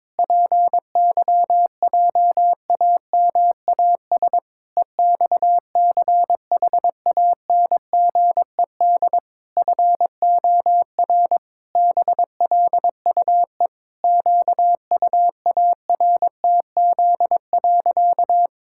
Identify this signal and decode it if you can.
Morse code, heard as a keyed tone, about 22 wpm, 700 Hz. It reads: PYJAMAS EXCHANGED FOR BLUE QUARTZ.
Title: Pangram Challenge v2 - 22wpm